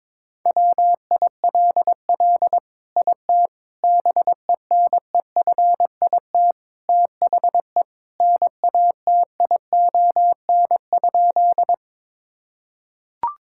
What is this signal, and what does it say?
Morse code, heard as a keyed tone, about 21 wpm, 700 Hz. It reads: WILL IT BENEFIT THE NATION?
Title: Sentences from Top 1000 Words - Review Part 6 - 22wpm